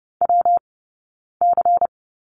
Morse code, heard as a keyed tone, about 30 wpm, spaced as for 9 wpm, 690 Hz. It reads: WC
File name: Lesson 19d - Letter w - Two Characters 30wpm